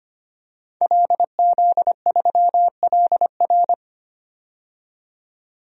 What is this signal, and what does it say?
Morse code, heard as a keyed tone, about 25 wpm, 700 Hz. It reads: LZ3LR
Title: DXCC-100 Call Signs - 5 Characters - 25wpm